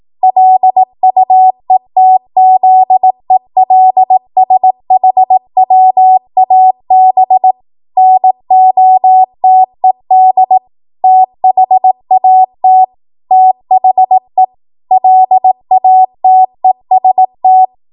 Morse code, heard as a keyed tone, about 18 wpm, 750 Hz. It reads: LUETZELSHWAB NOTED THAT THE LATEST
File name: Archive / 18 WPM / 18WPM